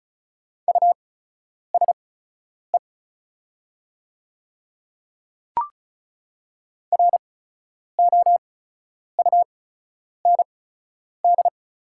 Morse code, heard as a keyed tone, about 34 wpm, spaced as for 10 wpm, 690 Hz. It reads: USE ROUND
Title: 8x Character Spacing - Top 300 Words - 35wpm